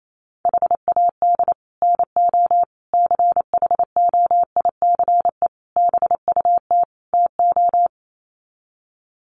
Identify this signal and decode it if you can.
Morse code, heard as a keyed tone, about 28 wpm, 700 Hz. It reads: HAD NO CHOICE BUT TO
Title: Sets of 5 Words 28wpm